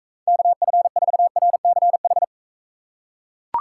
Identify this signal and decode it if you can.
Morse code, heard as a keyed tone, about 39 wpm, 700 Hz. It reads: KF4RCH